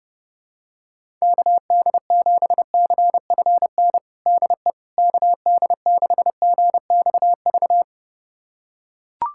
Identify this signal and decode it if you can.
Morse code, heard as a keyed tone, about 32 wpm, 700 Hz. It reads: KD7CFN DE KD6GXV
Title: USA Callsign DE Callsign 30wpm